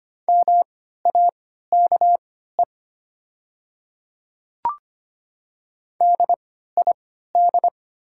Morse code, heard as a keyed tone, about 25 wpm, spaced as for 14 wpm, 700 Hz. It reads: MAKE DID
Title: 3x Character Spacing - Top 200 Words - 25wpm